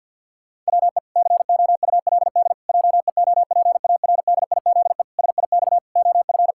Morse code, heard as a keyed tone, about 50 wpm, 700 Hz. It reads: WE COULD JEOPARDIZE SIX OF
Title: Pangram Challenge v2 - 50wpm